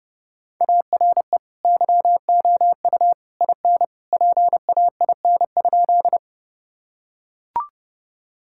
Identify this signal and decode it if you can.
Morse code, heard as a keyed tone, about 32 wpm, 700 Hz. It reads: ARE YOU IN PAIN?
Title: Sentences from Top 1600 Words 30wpm